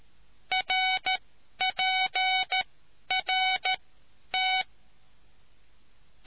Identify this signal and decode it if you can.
Morse code, heard as a keyed tone, about 13 wpm, spaced as for 10 wpm, 750 Hz. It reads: RPRT